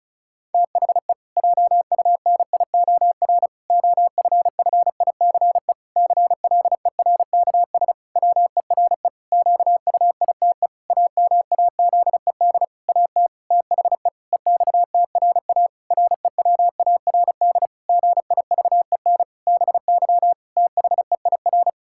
Morse code, heard as a keyed tone, about 36 wpm, 700 Hz. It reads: THE JUNIOR OFFICE CLERKS WERE QUITE AMAZED AT THE EXTRA REWARD GIVEN BY THEIR
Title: Pangram Challenge v2 - 35wpm